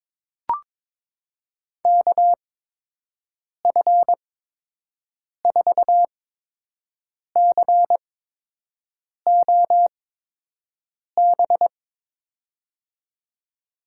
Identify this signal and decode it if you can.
Morse code, heard as a keyed tone, about 22 wpm, spaced as for 6 wpm, 700 Hz. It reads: KF4COB
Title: USA Call Signs - 6 Characters - 8x Character Spacing - 22wpm